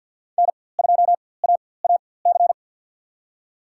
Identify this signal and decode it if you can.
Morse code, heard as a keyed tone, about 53 wpm, spaced as for 24 wpm, 700 Hz. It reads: N2AAC
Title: USA Call Signs - 5 Characters - 4x Character Spacing - 50wpm